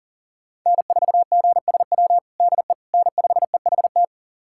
Morse code, heard as a keyed tone, about 40 wpm, 700 Hz. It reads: N4GSW DE N5EHT
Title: USA Callsign DE Callsign - 5 Characters - 40wpm